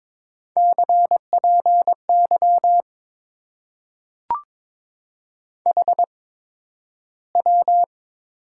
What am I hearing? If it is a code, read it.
Morse code, heard as a keyed tone, about 22 wpm, spaced as for 12 wpm, 700 Hz. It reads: CPY HW
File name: Top 90 QSO Elements - 8x Character Spacing - 22wpm